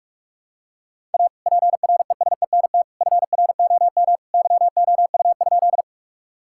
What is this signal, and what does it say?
Morse code, heard as a keyed tone, about 42 wpm, 700 Hz. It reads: A PRESENT FROM YOU?
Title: Sentences from Top 1000 Words - Review Part 5 - 45wpm